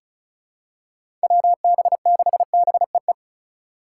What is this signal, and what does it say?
Morse code, heard as a keyed tone, about 36 wpm, 700 Hz. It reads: WB6BEE
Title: K1USN SST QSO Practice - 35wpm